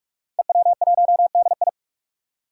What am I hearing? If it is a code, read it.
Morse code, heard as a keyed tone, about 45 wpm, 700 Hz. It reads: EW1DI
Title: Five Challenging Letter-Number Combinations 45wpm